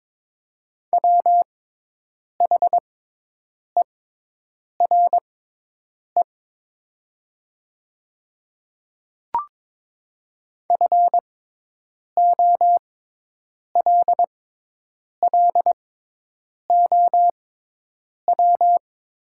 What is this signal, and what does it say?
Morse code, heard as a keyed tone, about 22 wpm, spaced as for 8 wpm, 700 Hz. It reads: WHERE FOLLOW
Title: 6x Character Spacing - Top 300 Words - 22wpm